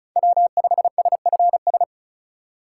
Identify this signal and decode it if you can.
Morse code, heard as a keyed tone, about 34 wpm, 700 Hz. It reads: W5SFS
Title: USA Call Signs - 5 Characters - 2x Character Spacing - 35wpm